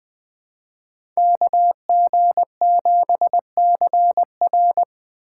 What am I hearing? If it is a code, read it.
Morse code, heard as a keyed tone, about 20 wpm, 700 Hz. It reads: KG7CR
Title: Speed Racing - USA 2x2 Call Signs 17wpm